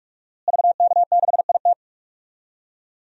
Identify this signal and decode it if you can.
Morse code, heard as a keyed tone, about 46 wpm, 700 Hz. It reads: VK6IT
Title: DXCC-100 Call Signs - 5 Characters - Repeated 3x - 45wpm